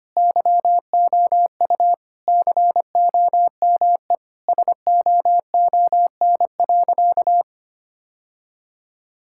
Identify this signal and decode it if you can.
Morse code, heard as a keyed tone, about 25 wpm, 700 Hz. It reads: YOU COME SOON.
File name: practice / 25wpm / Sentences with 2-Word Phrases - 25wpm